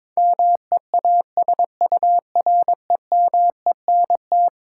Morse code, heard as a keyed tone, about 22 wpm, 700 Hz. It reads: MEASUREMENT
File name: practice / 22wpm / Eleven Letter Words - 22wpm